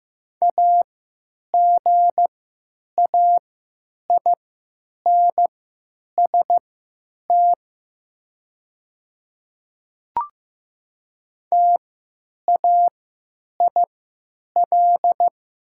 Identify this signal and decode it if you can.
Morse code, heard as a keyed tone, about 15 wpm, spaced as for 9 wpm, 700 Hz. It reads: AGAINST TAIL
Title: 3x Character Spacing - Top 500 Words - 15wpm